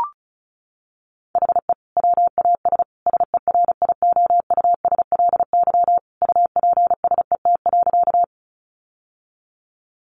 Morse code, heard as a keyed tone, about 34 wpm, 700 Hz. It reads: HE WAS SERIOUSLY UPSET.
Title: Sentences from Top 1200 Words 35wpm